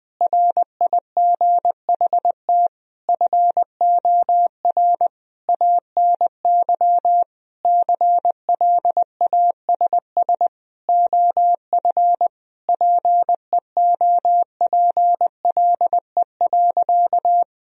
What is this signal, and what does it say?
Morse code, heard as a keyed tone, about 20 wpm, 700 Hz. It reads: RIGHT FOR ANY CLASS OF PEOPLE.
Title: Sentences from Top 1000 Words - Review Part 2 - 20wpm